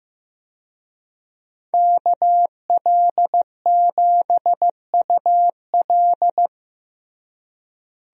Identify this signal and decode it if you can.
Morse code, heard as a keyed tone, about 16 wpm, 700 Hz. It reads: KL7UL
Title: USA Call Signs - 5 Characters - Repeated 3x - 15wpm